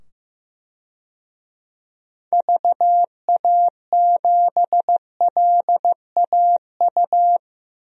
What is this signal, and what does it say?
Morse code, heard as a keyed tone, about 15 wpm, 700 Hz. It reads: VA7LAU
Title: DXCC-100 Call Signs - 6 Characters - 15wpm